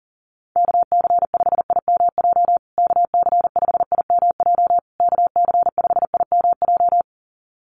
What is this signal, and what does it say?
Morse code, heard as a keyed tone, about 37 wpm, 700 Hz. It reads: KC5IMJ KC5IMJ KC5IMJ
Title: USA Call Signs Repeated 3x 40wpm